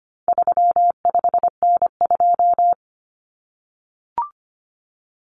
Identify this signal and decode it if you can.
Morse code, heard as a keyed tone, about 25 wpm, 700 Hz. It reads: 35N2
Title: Common Cut Numbers - 4-Digit 25wpm